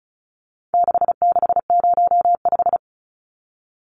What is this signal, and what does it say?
Morse code, heard as a keyed tone, about 34 wpm, 700 Hz. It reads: 6605
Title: Numbers - 4-Digit 35wpm